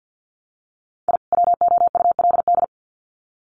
Morse code, heard as a keyed tone, about 48 wpm, 700 Hz. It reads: I WOULD
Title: Sets of 2 Words 50wpm